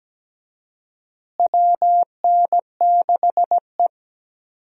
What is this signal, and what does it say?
Morse code, heard as a keyed tone, about 17 wpm, 700 Hz. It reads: WN6E
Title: USA Call Signs - 4 Characters - Repeated 2x - 17wpm